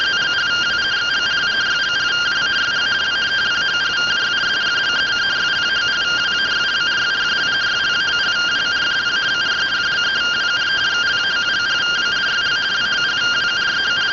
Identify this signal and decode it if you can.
Two-tone teleprinter signal, about 170 Hz shift.